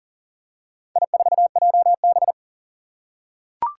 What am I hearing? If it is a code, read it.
Morse code, heard as a keyed tone, about 41 wpm, 700 Hz. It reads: I4JB
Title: Four Challenging Letter-Number Combinations 40wpm